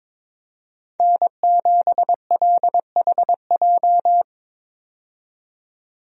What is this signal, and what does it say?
Morse code, heard as a keyed tone, about 22 wpm, 700 Hz. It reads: N7LHJ
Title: USA Call Signs - 5 Characters - 22wpm